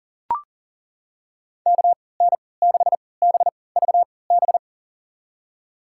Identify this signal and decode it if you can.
Morse code, heard as a keyed tone, about 40 wpm, spaced as for 23 wpm, 700 Hz. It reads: KN6BVB
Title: USA Call Signs - 6 Characters - 3x Character Spacing - 40wpm